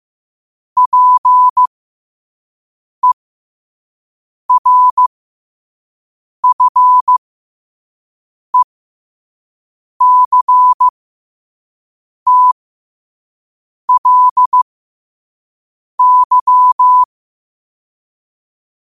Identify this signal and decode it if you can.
Morse code, heard as a keyed tone, about 15 wpm, spaced as for 5 wpm, 1000 Hz. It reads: PERFECTLY
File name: qotdfiles / 5wpm